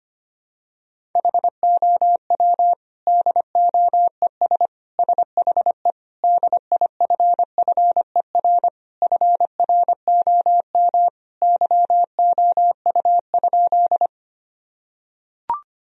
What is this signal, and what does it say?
Morse code, heard as a keyed tone, about 25 wpm, 700 Hz. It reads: HOW DOES SHE DIFFER FROM YOU?
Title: Sentences from Top 1000 Words - Review Part 1 - 25wpm